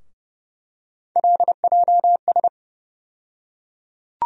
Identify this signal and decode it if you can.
Morse code, heard as a keyed tone, about 31 wpm, 700 Hz. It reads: LJS
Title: USA Call Sign Postfixes - 3 Letters 30wpm